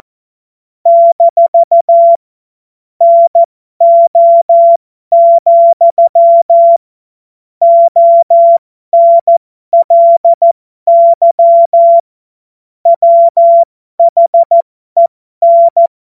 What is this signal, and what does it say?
Morse code, heard as a keyed tone, about 14 wpm, spaced as for 12 wpm, 680 Hz. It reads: - NO, ONLY WHEN
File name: Morsekurs / Witze